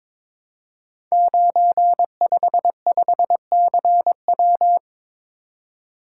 Morse code, heard as a keyed tone, about 22 wpm, 700 Hz. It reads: 955CW
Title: Code Groups - 5 Letter:Numbers - 22wpm